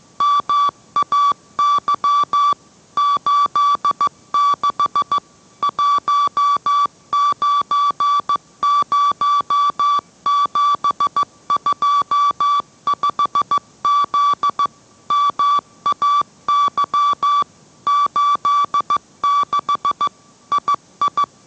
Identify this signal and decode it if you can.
Morse code, heard as a keyed tone, about 15 wpm, 1150 Hz. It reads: MAY 86 190725Z MAY 86 II